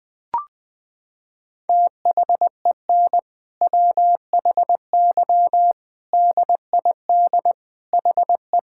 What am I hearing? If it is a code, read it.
Morse code, heard as a keyed tone, about 21 wpm, 700 Hz. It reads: THEN WHY DID HE
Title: Sentences from Top 1000 Words - Review Part 5 - 20wpm